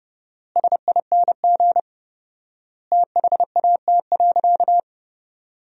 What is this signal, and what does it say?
Morse code, heard as a keyed tone, about 30 wpm, 700 Hz. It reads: SING THAT.
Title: 4x Word-Spacing - Sentences from Top 500 Words - 30wpm